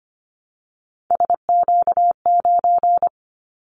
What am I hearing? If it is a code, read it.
Morse code, heard as a keyed tone, about 25 wpm, 700 Hz. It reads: SQ9
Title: Callsign Prefixes for Top 25 DXCC - Repeated 3x - 25wpm